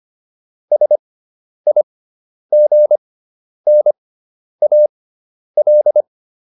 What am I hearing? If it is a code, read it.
Morse code, heard as a keyed tone, about 25 wpm, spaced as for 10 wpm, 600 Hz. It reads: SIGNAL